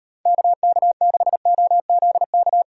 Morse code, heard as a keyed tone, about 31 wpm, 700 Hz. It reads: KK6OZK